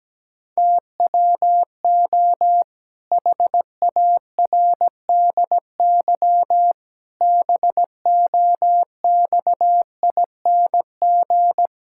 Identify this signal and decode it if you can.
Morse code, heard as a keyed tone, about 17 wpm, 700 Hz. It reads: TWO HARDY BOXING
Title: Pangram Challenge v2 - 17wpm